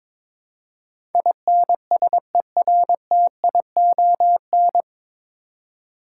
Morse code, heard as a keyed tone, about 22 wpm, 700 Hz. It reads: INSERTION
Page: Morse Code Ninja practice 22wpm